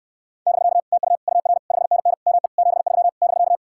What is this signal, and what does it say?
Morse code, heard as a keyed tone, about 67 wpm, 700 Hz. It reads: <BK> TU UR 5NN ME 73 <BK>